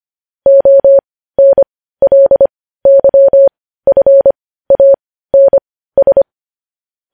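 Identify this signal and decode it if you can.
Morse code, heard as a keyed tone, about 25 wpm, spaced as for 15 wpm, 550 Hz. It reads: ONLYFANS
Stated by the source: Morse code sending of news headlines at 15 wpm.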